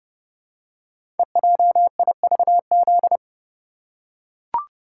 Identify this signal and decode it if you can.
Morse code, heard as a keyed tone, about 30 wpm, 700 Hz. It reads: EJIVZ
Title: Code Groups - 5 Letters - 30wpm